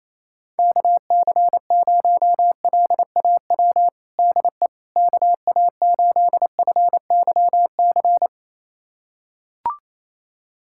Morse code, heard as a keyed tone, about 28 wpm, 700 Hz. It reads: KC0LAW DE KA8FYC
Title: USA Callsign DE Callsign - 6 Characters - 28wpm